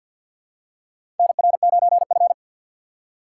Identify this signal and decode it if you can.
Morse code, heard as a keyed tone, about 51 wpm, 700 Hz. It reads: NR9F
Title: USA Call Signs - 4 Characters - 2x Character Spacing - 50wpm